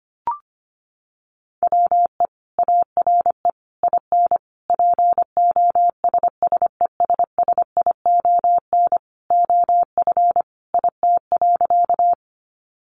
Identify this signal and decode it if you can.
Morse code, heard as a keyed tone, about 25 wpm, 700 Hz. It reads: WE ARE IN POSSESSION OF IT.
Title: Sentences from Top 1400 Words 25wpm